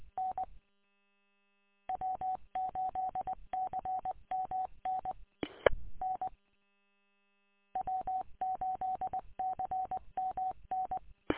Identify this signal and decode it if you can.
Morse code, heard as a keyed tone, about 20 wpm, 750 Hz. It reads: N W8CMN N W8CMN
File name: w8cmn- -combined